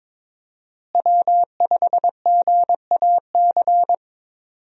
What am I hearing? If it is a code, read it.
Morse code, heard as a keyed tone, about 22 wpm, 700 Hz. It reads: W5GAC
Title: USA Call Signs - 5 Characters - Repeated 2x - 22wpm